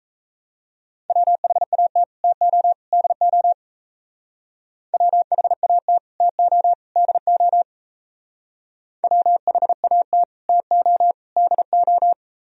Morse code, heard as a keyed tone, about 34 wpm, 700 Hz. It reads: WHAT TO DO WHAT TO DO WHAT TO DO
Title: Speed-Racing - Sets of 3 Words 28wpm